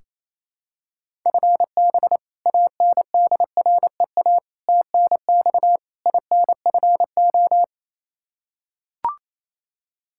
Morse code, heard as a keyed tone, about 27 wpm, 700 Hz. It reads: FB ANDREA TNX INFO
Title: Practice QSO - Exchange 3 - More Info - 28wpm